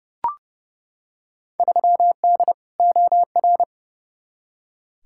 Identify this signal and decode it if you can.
Morse code, heard as a keyed tone, about 32 wpm, 700 Hz. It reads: 3D OR
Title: Field Day Exchanges Encore - 30wpm